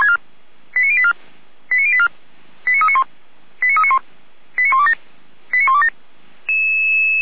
Five-tone selective call (ZVEI), heard as quick runs of five tones, tones of about 70 ms. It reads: 89084 89084 89391 89391 89127 89127